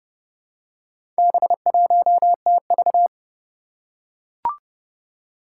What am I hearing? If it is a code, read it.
Morse code, heard as a keyed tone, about 30 wpm, 700 Hz. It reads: B1TV
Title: Four Challenging Letter-Number Combinations 30wpm